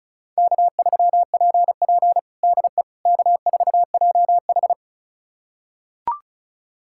Morse code, heard as a keyed tone, about 36 wpm, 700 Hz. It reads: K3PP DE K4JH
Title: USA Callsign DE Callsign - 4 Characters - 35wpm